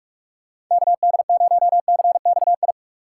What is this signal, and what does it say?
Morse code, heard as a keyed tone, about 46 wpm, 700 Hz. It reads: KD0CXI
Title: USA Call Signs - 2x Character Spacing - 45wpm